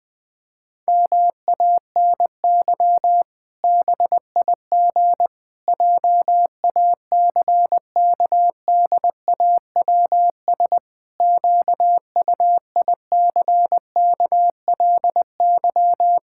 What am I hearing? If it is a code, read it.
Morse code, heard as a keyed tone, about 20 wpm, 700 Hz. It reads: MANY BIG JACKDAWS QUICKLY
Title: Pangram Challenge v2 - 20wpm